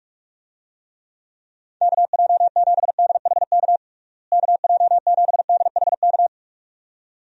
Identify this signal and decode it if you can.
Morse code, heard as a keyed tone, about 46 wpm, 700 Hz. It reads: KJ7DHK KJ7DHK
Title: USA Call Signs - 6 Characters - Repeated 2x - 45wpm